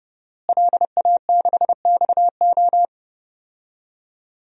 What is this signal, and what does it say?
Morse code, heard as a keyed tone, about 30 wpm, 700 Hz. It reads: LA6XO
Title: DXCC-100 Call Signs - 5 Characters - Repeated 3x - 30wpm